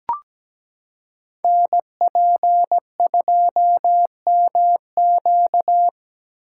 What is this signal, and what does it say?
Morse code, heard as a keyed tone, about 17 wpm, 700 Hz. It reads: NP2MQ
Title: Code Groups - 5 Letter:Numbers - 17wpm